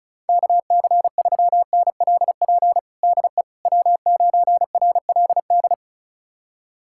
Morse code, heard as a keyed tone, about 34 wpm, 700 Hz. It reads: KC3NLP DE W9RLD
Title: USA Callsign DE Callsign 35wpm